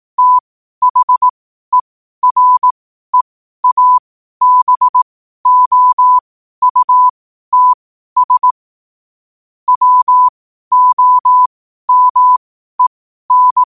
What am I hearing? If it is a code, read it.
Morse code, heard as a keyed tone, about 18 wpm, spaced as for 12 wpm, 1000 Hz. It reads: THEREABOUTS WOMEN
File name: qotdfiles / 12wpm